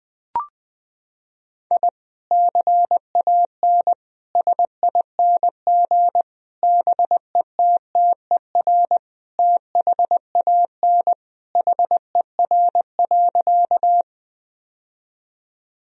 Morse code, heard as a keyed tone, about 20 wpm, 700 Hz. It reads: I CAN SING BETTER THAN HER.
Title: Sentences from Top 1000 Words - Review Part 2 - 20wpm